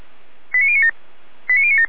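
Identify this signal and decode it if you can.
Five-tone selective call (ZVEI), heard as quick runs of five tones, tones of about 70 ms. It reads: 89097
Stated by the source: ZVEI five-tone selective call received on SDR.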